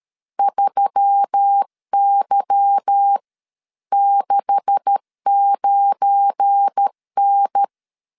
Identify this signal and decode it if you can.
Morse code, heard as a keyed tone, about 13 wpm, spaced as for 11 wpm, 800 Hz. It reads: 3Y69N
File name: LEZIONI / 9 286 12wpm 2305